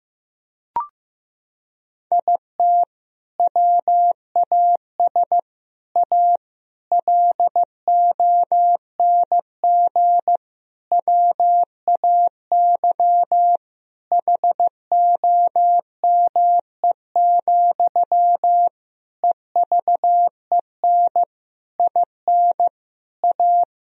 Morse code, heard as a keyed tone, about 16 wpm, 700 Hz. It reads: IT WAS A LONG WAY HOME, EVEN IN A
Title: Sentences from Top 1000 Words - Review Part 2 - 15wpm